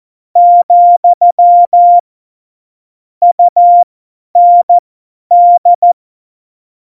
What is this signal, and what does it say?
Morse code, heard as a keyed tone, about 14 wpm, spaced as for 10 wpm, 700 Hz. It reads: , UND
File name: Morsekurs / Witze / Witz 12. Januar 2021, 50 F70